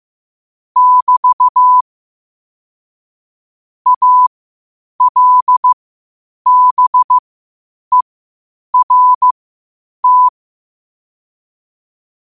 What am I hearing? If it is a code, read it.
Morse code, heard as a keyed tone, about 15 wpm, spaced as for 8 wpm, 1000 Hz. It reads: <BT> ALBERT
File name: qotdfiles / 8wpm